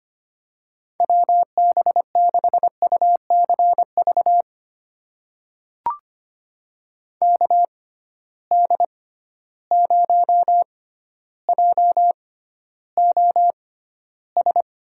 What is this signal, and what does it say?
Morse code, heard as a keyed tone, about 25 wpm, spaced as for 13 wpm, 700 Hz. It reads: WB6UCV KD0JOS
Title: USA Call Signs - 6 Characters - 6x Character Spacing - 25wpm